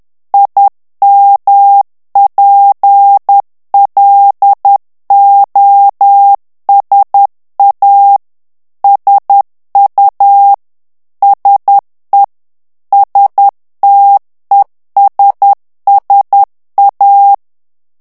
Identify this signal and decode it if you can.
Morse code, heard as a keyed tone, about 10 wpm, 800 Hz. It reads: IMPLOSA SU SE STESSA
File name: morse mp3 / ITALIEN texte clair 1 vitesse 10 mots minute